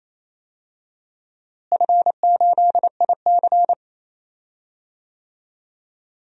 Morse code, heard as a keyed tone, about 28 wpm, 700 Hz. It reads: F8IC
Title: DXCC-100 Call Signs - 4 Characters - Repeated 2x - 28wpm